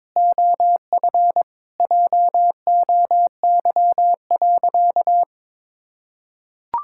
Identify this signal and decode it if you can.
Morse code, heard as a keyed tone, about 22 wpm, 700 Hz. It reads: OF JOY.
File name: practice / 22wpm / Sentences with 3-Word Phrases - 22wpm